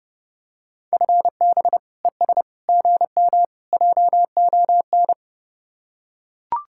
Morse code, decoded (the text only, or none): FB ES GM JON